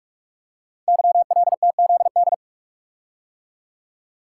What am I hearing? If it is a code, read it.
Morse code, heard as a keyed tone, about 44 wpm, 700 Hz. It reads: YLTZD
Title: Code Groups - 5 Letter:Numbers - 45wpm